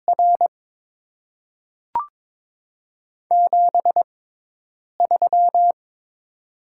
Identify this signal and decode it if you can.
Morse code, heard as a keyed tone, about 23 wpm, spaced as for 8 wpm, 700 Hz. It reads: R 73
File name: Top 20 QSO Elements - 6x Character Spacing - 22wpm